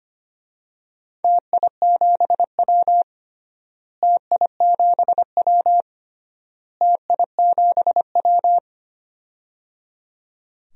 Morse code, heard as a keyed tone, about 25 wpm, 700 Hz. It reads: TI7W TI7W TI7W
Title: DXCC-100 Call Signs - 4 Characters - Repeated 3x - 25wpm